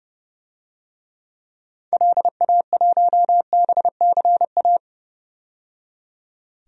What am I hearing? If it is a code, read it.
Morse code, heard as a keyed tone, about 30 wpm, 700 Hz. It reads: LA1BCA
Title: DXCC-100 Call Signs - 6 Characters - 30wpm